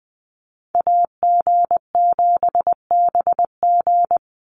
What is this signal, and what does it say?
Morse code, heard as a keyed tone, about 20 wpm, 700 Hz. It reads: AG7BG
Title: USA Call Signs 20wpm